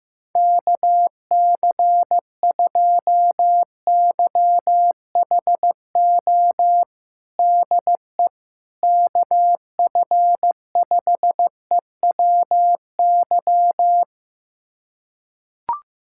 Morse code, heard as a keyed tone, about 15 wpm, 700 Hz. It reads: KC2YHO DE KF5EWY
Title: USA Callsign DE Callsign - 6 Characters - 15wpm